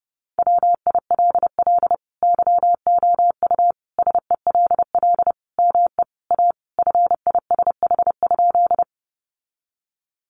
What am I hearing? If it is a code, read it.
Morse code, heard as a keyed tone, about 30 wpm, 700 Hz. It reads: WILL YOU SELL ME A FISH?
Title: Sentences from Top 1100 Words 30wpm